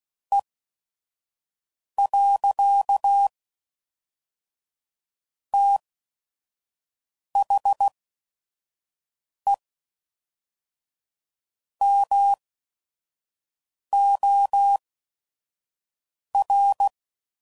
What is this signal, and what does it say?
Morse code, heard as a keyed tone, about 16 wpm, spaced as for 5 wpm, 800 Hz. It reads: E. THE MOR
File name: CW Code Groups @5WPM / Technical Subjects / XC